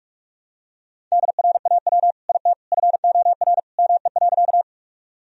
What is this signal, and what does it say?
Morse code, heard as a keyed tone, about 42 wpm, 700 Hz. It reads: DRAW IT FOR ME.
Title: Sentences from Top 1000 Words - Review Part 1 - 45wpm